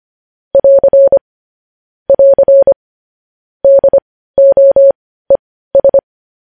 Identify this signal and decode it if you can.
Morse code, heard as a keyed tone, about 25 wpm, spaced as for 15 wpm, 550 Hz. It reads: <AR> <AR> DOES
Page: news headlines in morse code at 15 wpm